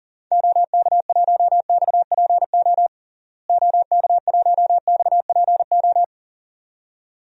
Morse code, heard as a keyed tone, about 40 wpm, 700 Hz. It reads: OK1XPO OK1XPO
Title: DXCC-100 Call Signs - 6 Characters - Repeated 2x - 40wpm